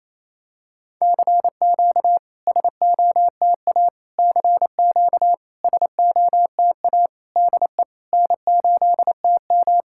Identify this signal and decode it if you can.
Morse code, heard as a keyed tone, about 28 wpm, 700 Hz. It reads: CQ SOTA CQ SOTA DE N8TM
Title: SOTA QSO Practice 28wpm